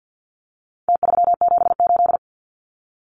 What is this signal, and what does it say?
Morse code, heard as a keyed tone, about 50 wpm, 700 Hz. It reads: T378
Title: Numbers - Contest Style - 4-Digit 50wpm